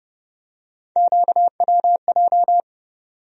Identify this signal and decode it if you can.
Morse code, heard as a keyed tone, about 30 wpm, 700 Hz. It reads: QWJ
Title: USA Call Sign Postfixes - 3 Letters 30wpm